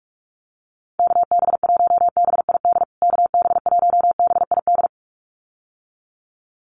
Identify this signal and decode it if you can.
Morse code, heard as a keyed tone, about 44 wpm, 700 Hz. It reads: KB1BID KB1BID
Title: USA Call Signs Repeated 2x 45wpm